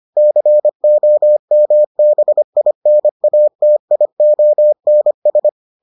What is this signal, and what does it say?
Morse code, heard as a keyed tone, about 25 wpm, 600 Hz. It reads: COMBINATIONS